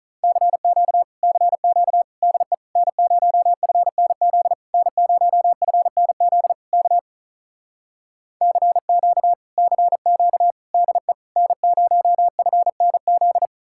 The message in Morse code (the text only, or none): CQ CQ DE N0FNZ N0FNZ K CQ CQ DE N0FNZ